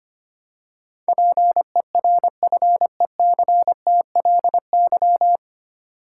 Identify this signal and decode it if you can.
Morse code, heard as a keyed tone, about 25 wpm, 700 Hz. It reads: PERFECTLY
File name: practice / 25wpm / 9 Letter Words - First 3 Letters Given - 25wpm